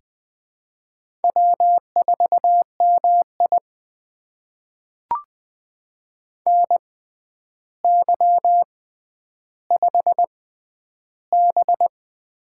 Morse code, heard as a keyed tone, about 20 wpm, spaced as for 10 wpm, 700 Hz. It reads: W4MI NY5B
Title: USA Call Signs - 4 Characters - 6x Character Spacing - 20wpm